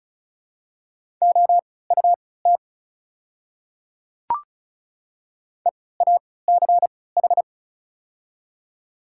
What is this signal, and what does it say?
Morse code, heard as a keyed tone, about 36 wpm, spaced as for 20 wpm, 700 Hz. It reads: OUT EACH
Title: 3x Character Spacing - Top 300 Words - 35wpm